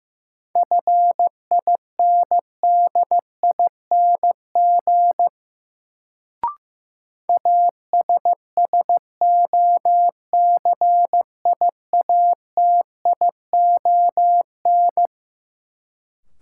Morse code, heard as a keyed tone, about 15 wpm, 700 Hz. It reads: FINDING ASSOCIATION
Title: Sentences from Top 1100 Words - New Word Drill - 15wpm